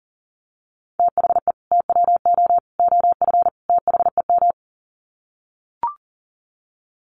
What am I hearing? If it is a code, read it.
Morse code, heard as a keyed tone, about 40 wpm, 700 Hz. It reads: THE TWO OF THEM
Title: Sets of 4 Words 40wpm